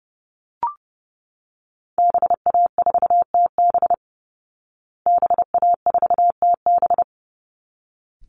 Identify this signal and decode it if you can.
Morse code, heard as a keyed tone, about 30 wpm, 700 Hz. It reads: BA4TB BA4TB
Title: Call Signs Repeated 3x - 30wpm